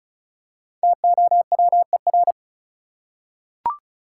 Morse code, heard as a keyed tone, about 36 wpm, 700 Hz. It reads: TOWER
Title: Five Letter Words - 35wpm